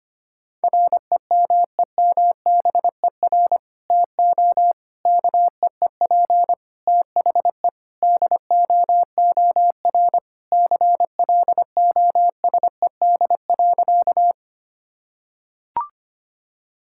Morse code, heard as a keyed tone, about 25 wpm, 700 Hz. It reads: REMEMBER TO KEEP THE DOOR CLOSED.
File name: practice / 25wpm / Sentences with 3-Word Phrases - 25wpm